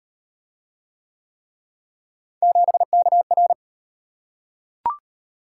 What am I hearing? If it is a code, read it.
Morse code, heard as a keyed tone, about 31 wpm, 700 Hz. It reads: 7KR
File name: Speed-Racing - Three Letter-Number Combinations - 25wpm